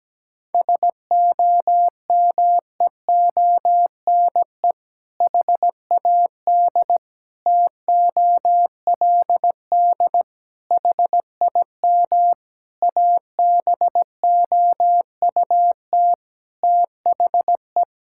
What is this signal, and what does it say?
Morse code, heard as a keyed tone, about 17 wpm, 700 Hz. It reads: SOMEONE HAD TOLD HIM ABOUT THE
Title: Sentences from Top 1000 Words - Review Part 6 - 17wpm